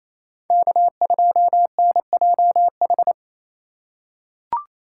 Morse code, heard as a keyed tone, about 28 wpm, 700 Hz. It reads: K2NJH
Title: USA Call Signs - 5 Characters - 2x Character Spacing - 28wpm